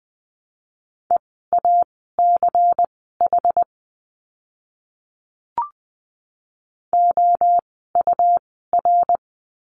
Morse code, heard as a keyed tone, about 19 wpm, spaced as for 14 wpm, 700 Hz. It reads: EACH OUR
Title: 244 Lesson 39f - Letter z - Words 20wpm